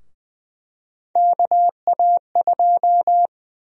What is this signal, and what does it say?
Morse code, heard as a keyed tone, about 20 wpm, 700 Hz. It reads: KA2
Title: USA Call Sign Prefixes - Repeated 2x - 20wpm